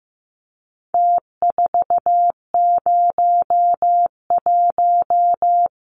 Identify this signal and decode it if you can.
Morse code, heard as a keyed tone, about 15 wpm, 700 Hz. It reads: T401
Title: Numbers - Contest Style - 4-Digit 15wpm